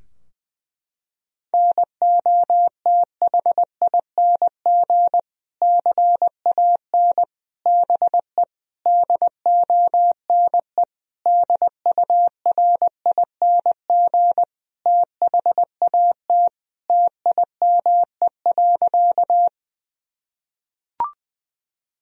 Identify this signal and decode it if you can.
Morse code, heard as a keyed tone, about 20 wpm, 700 Hz. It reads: NOTHING CAN BE DONE DURING THAT TIME.
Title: Sentences from Top 1000 Words - Review Part 2 - 20wpm